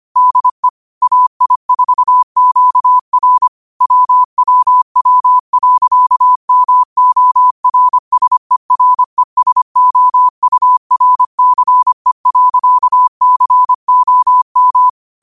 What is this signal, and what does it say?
Morse code, heard as a keyed tone, about 25 wpm, 1000 Hz. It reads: DE AI4QR WWW.MORSERESOURCE.COM